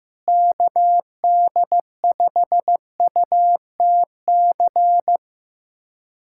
Morse code, heard as a keyed tone, about 15 wpm, 700 Hz. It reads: KD5UTC
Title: USA Callsign DE Callsign 15wpm